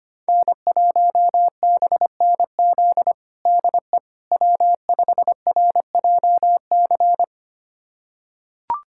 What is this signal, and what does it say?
Morse code, heard as a keyed tone, about 25 wpm, 700 Hz. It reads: N1BNZ DE W5RJC
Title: USA Callsign DE Callsign - 5 Characters - 25wpm